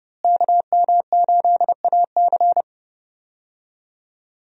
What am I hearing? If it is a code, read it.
Morse code, heard as a keyed tone, about 32 wpm, 700 Hz. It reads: KM8AC
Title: USA Call Signs - 5 Characters - Repeated 3x - 30wpm